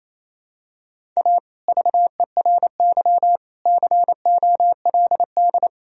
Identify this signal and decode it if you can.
Morse code, heard as a keyed tone, about 27 wpm, 700 Hz. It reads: A VERY COLD
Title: Sentences from Top 1000 Words - Review Part 1 - 28wpm